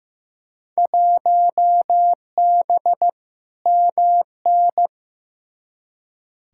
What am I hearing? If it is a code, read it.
Morse code, heard as a keyed tone, about 15 wpm, 700 Hz. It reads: 1B MN
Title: Field Day Exchanges Encore - 15wpm